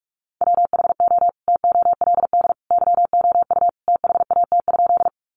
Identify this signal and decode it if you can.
Morse code, heard as a keyed tone, about 46 wpm, 700 Hz. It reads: WHO TOLD YOU THAT?
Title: Sentences from Top 400 Words 45wpm